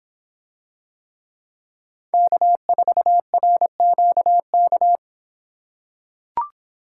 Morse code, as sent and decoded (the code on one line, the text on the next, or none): -.- ....- .-. --.- -.-
K4RQK